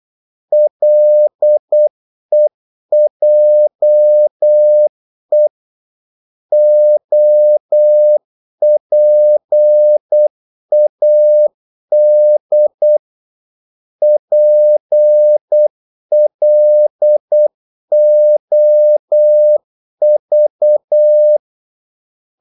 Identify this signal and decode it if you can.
Morse code, heard as a keyed tone, about 8 wpm, 600 Hz. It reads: LEJE OPAD PLOV
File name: Korte ord DK 8wpm